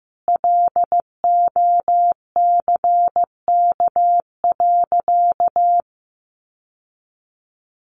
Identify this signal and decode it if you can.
Morse code, heard as a keyed tone, about 15 wpm, 700 Hz. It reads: LOCK.
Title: Sentences from Top 1300 Words 15wpm